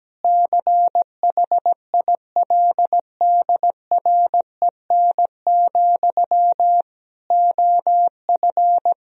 Morse code, heard as a keyed tone, about 17 wpm, 700 Hz. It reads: CHILDREN, OF
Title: Sentences from Top 1000 Words - Review Part 1 - 17wpm